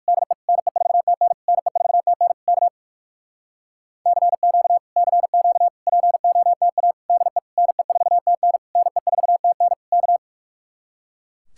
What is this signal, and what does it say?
Morse code, heard as a keyed tone, about 59 wpm, 700 Hz. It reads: DE NE4TN NE4TN K CQ CQ POTA DE NE4TN NE4TN K